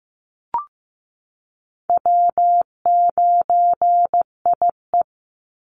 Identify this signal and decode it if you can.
Morse code, heard as a keyed tone, about 15 wpm, 700 Hz. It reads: W9IE
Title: Easy Call Signs Repeated 2x - 15wpm